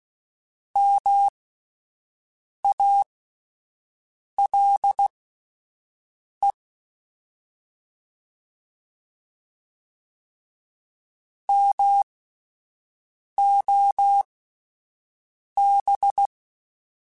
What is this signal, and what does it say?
Morse code, heard as a keyed tone, about 16 wpm, spaced as for 5 wpm, 800 Hz. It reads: MALE MOB
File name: Alphabetical / Alphabetical mp3s / Ms @5